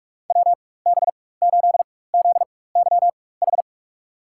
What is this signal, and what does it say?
Morse code, heard as a keyed tone, about 44 wpm, spaced as for 21 wpm, 700 Hz. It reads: WB8ZYH
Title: USA Call Signs - 6 Characters - 4x Character Spacing - 45wpm